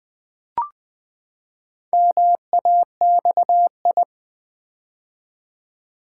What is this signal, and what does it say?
Morse code, heard as a keyed tone, about 20 wpm, 700 Hz. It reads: MAXI
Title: Build A Word - 4 Letters At A Time - 8-Letter Words - 20wpm